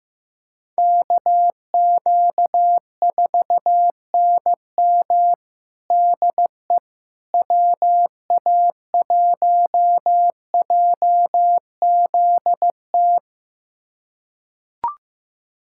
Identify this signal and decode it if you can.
Morse code, heard as a keyed tone, about 15 wpm, 700 Hz. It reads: KQ4NM DE WA1JZT
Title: USA Callsign DE Callsign 15wpm